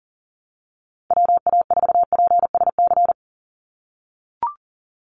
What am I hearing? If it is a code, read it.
Morse code, heard as a keyed tone, about 40 wpm, 700 Hz. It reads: WA4PSC
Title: Call Signs - 40wpm